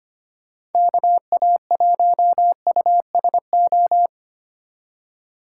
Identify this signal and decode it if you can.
Morse code, heard as a keyed tone, about 33 wpm, 700 Hz. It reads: KA1USO